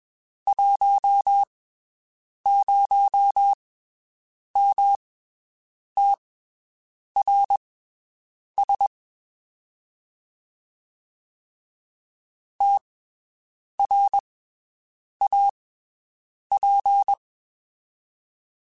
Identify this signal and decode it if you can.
Morse code, heard as a keyed tone, about 22 wpm, spaced as for 7 wpm, 800 Hz. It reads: 10MTRS TRAP
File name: (spaced) / QSOs Spaced-mp3s / WD0FBT 1 @20 sp